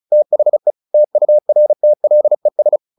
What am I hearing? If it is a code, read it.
Morse code, heard as a keyed tone, about 35 wpm, 600 Hz. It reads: THE TURTLES
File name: the old man and the sea 2 35wpm